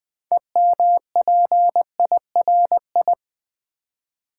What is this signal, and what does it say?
Morse code, heard as a keyed tone, about 20 wpm, 700 Hz. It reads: EMPIRI
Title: Build A Word - 3 Letters At A Time - 9-Letter Words - 20wpm